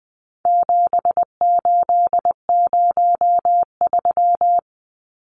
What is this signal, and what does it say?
Morse code, heard as a keyed tone, about 20 wpm, 700 Hz. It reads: 7803
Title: Numbers - 4-Digit 20wpm